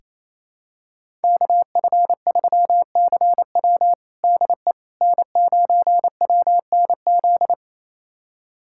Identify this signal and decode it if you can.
Morse code, heard as a keyed tone, about 28 wpm, 700 Hz. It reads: KF3CW DE N9WNZ
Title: USA Callsign DE Callsign - 5 Characters - 28wpm